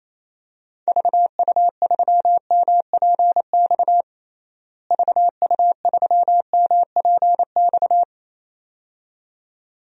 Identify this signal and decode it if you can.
Morse code, heard as a keyed tone, about 28 wpm, 700 Hz. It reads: VU3MPX VU3MPX
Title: DXCC-100 Call Signs - 6 Characters - Repeated 2x - 28wpm